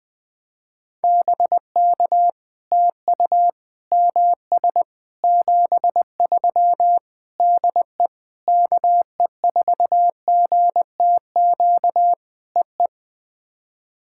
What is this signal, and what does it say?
Morse code, heard as a keyed tone, about 20 wpm, 700 Hz. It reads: BK TU MS 73 DE KE4GTQ EE
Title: POTA QSO Practice - 20wpm